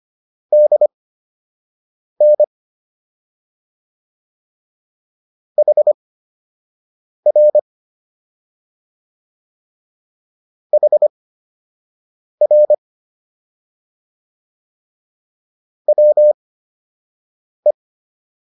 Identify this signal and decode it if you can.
Morse code, heard as a keyed tone, about 25 wpm, spaced as for 6 wpm, 600 Hz. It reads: DN HR HR WE